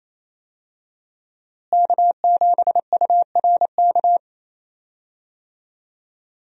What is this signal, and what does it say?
Morse code, heard as a keyed tone, about 28 wpm, 700 Hz. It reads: K7URK
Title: USA Call Signs - 5 Characters - Repeated 2x - 28wpm